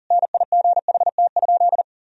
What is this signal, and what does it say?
Morse code, heard as a keyed tone, about 40 wpm, 700 Hz. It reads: NIGHT?
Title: Sentences from Top 300 Words and Abbreviations 40wpm